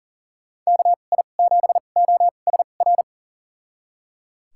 Morse code, heard as a keyed tone, about 38 wpm, spaced as for 28 wpm, 700 Hz. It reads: KI7OSR
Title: USA Call Signs - 6 Characters - 2x Character Spacing - 40wpm